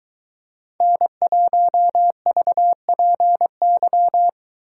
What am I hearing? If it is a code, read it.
Morse code, heard as a keyed tone, about 22 wpm, 700 Hz. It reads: N1VPY